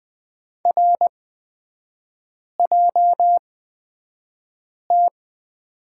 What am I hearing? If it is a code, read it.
Morse code, heard as a keyed tone, about 20 wpm, spaced as for 5 wpm, 700 Hz. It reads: RJT
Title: 250 Lesson 40e - BK - Three Characters 20wpm